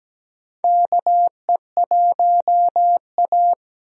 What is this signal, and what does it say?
Morse code, heard as a keyed tone, about 17 wpm, 700 Hz. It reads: KE1A
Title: USA Call Signs - 4 Characters - 17wpm